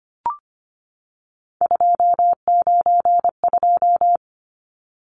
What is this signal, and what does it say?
Morse code, heard as a keyed tone, about 25 wpm, 700 Hz. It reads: 292
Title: Numbers - 3-Digit 25wpm